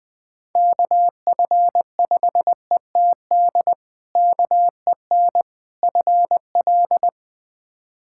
Morse code, heard as a keyed tone, about 21 wpm, 700 Hz. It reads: KF5ETD KEN FL
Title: K1USN SST QSO Practice - 20wpm